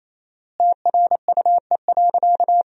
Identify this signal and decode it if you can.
Morse code, heard as a keyed tone, about 28 wpm, 700 Hz. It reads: TRUE.
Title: 4x Word-Spacing - Sentences from Top 400 Words - 28wpm